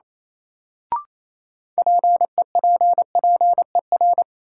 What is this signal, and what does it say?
Morse code, heard as a keyed tone, about 28 wpm, 700 Hz. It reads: PEPPER
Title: Sets of 5 Words - New Word Drill - 28wpm